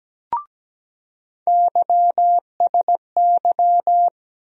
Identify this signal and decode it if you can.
Morse code, heard as a keyed tone, about 17 wpm, 700 Hz. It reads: YSY
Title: USA Call Sign Postfixes - 3 Letters 17wpm